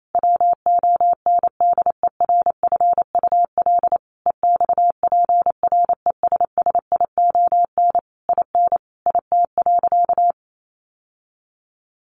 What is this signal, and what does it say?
Morse code, heard as a keyed tone, about 28 wpm, 700 Hz. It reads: WONDERFUL EXPRESSION IN IT.
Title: Sentences from Top 1200 Words 28wpm